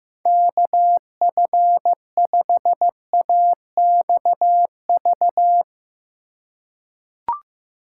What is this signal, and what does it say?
Morse code, heard as a keyed tone, about 15 wpm, 700 Hz. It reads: KF5AXV
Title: USA Call Signs - 6 Characters - 3x Character Spacing - 15wpm